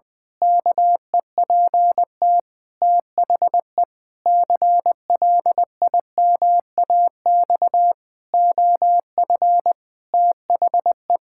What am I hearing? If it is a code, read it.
Morse code, heard as a keyed tone, about 20 wpm, 700 Hz. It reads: KEPT THE CLIMAX OF THE
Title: Pangram Challenge v2 - 20wpm